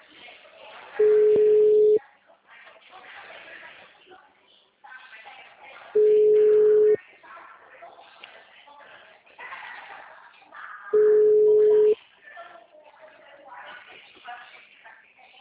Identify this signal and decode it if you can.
Ringback tone (Europe, 425 Hz), with speech.